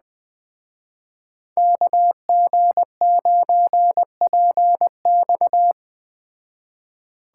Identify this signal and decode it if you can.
Morse code, heard as a keyed tone, about 19 wpm, 700 Hz. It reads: KG9PX